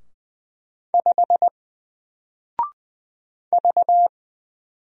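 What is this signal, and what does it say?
Morse code, heard as a keyed tone, about 20 wpm, 700 Hz. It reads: 5 V
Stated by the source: Single letter and number Morse drill at 20 wpm, from N4LCD.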